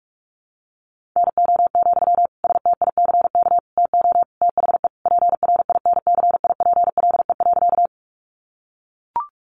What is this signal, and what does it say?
Morse code, heard as a keyed tone, about 44 wpm, 700 Hz. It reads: NO, STICK TO THE PRINCIPLE.
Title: Sentences from Top 1400 Words 45wpm